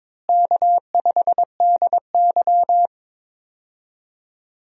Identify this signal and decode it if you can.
Morse code, heard as a keyed tone, about 22 wpm, 700 Hz. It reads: K5DY
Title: USA Call Signs - 4 Characters - Repeated 3x - 22wpm